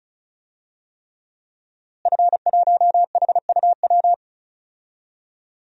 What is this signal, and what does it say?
Morse code, heard as a keyed tone, about 34 wpm, 700 Hz. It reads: F1HUW
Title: DXCC-100 Call Signs - 5 Characters - Repeated 3x - 35wpm